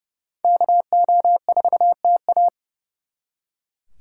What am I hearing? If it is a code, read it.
Morse code, heard as a keyed tone, about 37 wpm, 700 Hz. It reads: KO4TA
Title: Speed Racing - USA Call Signs 30wpm